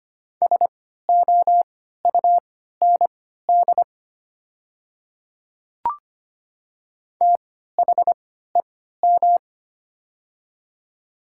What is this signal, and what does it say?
Morse code, heard as a keyed tone, about 25 wpm, spaced as for 14 wpm, 700 Hz. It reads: SOUND THEM
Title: 3x Character Spacing - Top 100 Words - 25wpm